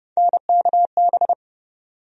Morse code, heard as a keyed tone, about 30 wpm, 700 Hz. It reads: NKB
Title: USA Call Sign Postfixes - 3 Letters 30wpm